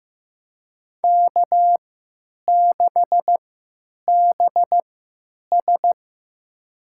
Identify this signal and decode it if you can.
Morse code, heard as a keyed tone, about 15 wpm, spaced as for 8 wpm, 700 Hz. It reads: K6BS
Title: USA Call Signs - 4 Characters - 3x Character Spacing - 15wpm